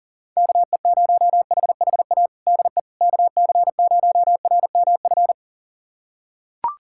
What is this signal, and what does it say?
Morse code, heard as a keyed tone, about 40 wpm, 700 Hz. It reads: KE0HHA DE KC0RMF
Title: USA Callsign DE Callsign 40wpm